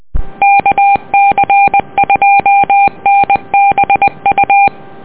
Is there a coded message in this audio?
Yes: Morse code from 0.4 to 4.7 s, heard as a keyed tone, about 20 wpm, 800 Hz. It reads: KC2NBU